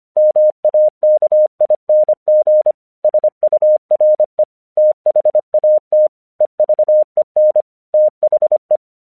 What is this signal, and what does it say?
Morse code, heard as a keyed tone, about 26 wpm, 610 Hz. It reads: MAKING SURE THAT EVEN THE
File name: 1 English 25wpm / AQC11